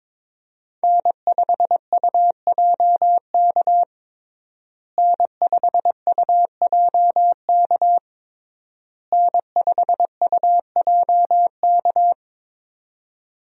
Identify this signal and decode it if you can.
Morse code, heard as a keyed tone, about 21 wpm, 700 Hz. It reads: N5UJK N5UJK N5UJK
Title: USA Call Signs - 5 Characters - Repeated 3x - 22wpm